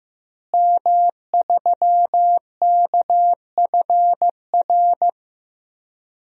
Morse code, heard as a keyed tone, about 15 wpm, 700 Hz. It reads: M3KFR
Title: DXCC-100 Call Signs - 5 Characters - Repeated 2x - 15wpm